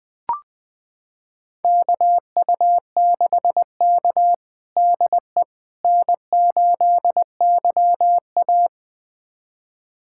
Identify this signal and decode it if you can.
Morse code, heard as a keyed tone, about 20 wpm, 700 Hz. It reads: KU6K DE N8YA
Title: USA Callsign DE Callsign - 4 Characters - 20wpm